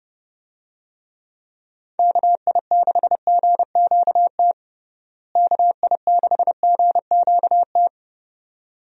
Morse code, heard as a keyed tone, about 30 wpm, 700 Hz. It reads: KI6GQT KI6GQT
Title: USA Call Signs - 6 Characters - Repeated 2x - 30wpm